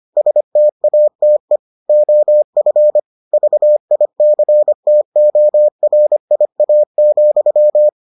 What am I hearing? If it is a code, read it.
Morse code, heard as a keyed tone, about 25 wpm, 600 Hz. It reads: STATE OF VICTORIA,